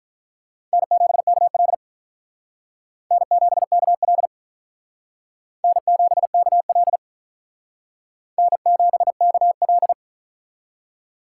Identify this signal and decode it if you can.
Morse code, heard as a keyed tone, about 45 wpm, 700 Hz. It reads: N7KL N7KL N7KL N7KL